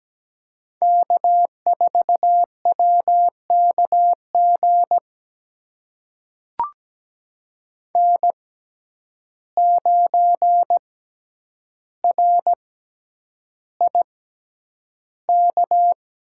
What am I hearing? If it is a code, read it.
Morse code, heard as a keyed tone, about 17 wpm, spaced as for 9 wpm, 700 Hz. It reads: K4WKG N9RIK
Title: USA Call Signs - 5 Characters - 6x Character Spacing - 17wpm